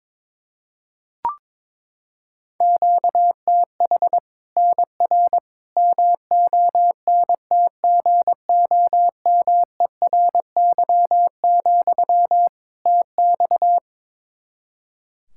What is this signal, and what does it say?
Morse code, heard as a keyed tone, about 21 wpm, 700 Hz. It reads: QTH NR MONTGOMERY, TX
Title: Practice QSO - Exchange 1 - Beginning - 22wpm